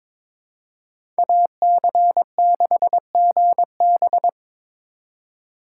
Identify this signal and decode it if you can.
Morse code, heard as a keyed tone, about 22 wpm, 700 Hz. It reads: AC6GB
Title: USA Call Signs - 5 Characters - 3x Character Spacing - 22wpm